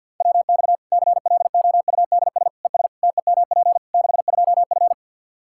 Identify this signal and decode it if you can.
Morse code, heard as a keyed tone, about 49 wpm, 700 Hz. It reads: WX CLOUDS ES TEMP 62F
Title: K1USN SST QSO Practice 50wpm